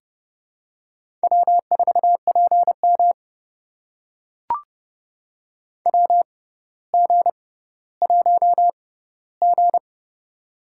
Morse code, heard as a keyed tone, about 30 wpm, spaced as for 15 wpm, 700 Hz. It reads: W4PM WG1G
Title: USA Call Signs - 4 Characters - 6x Character Spacing - 30wpm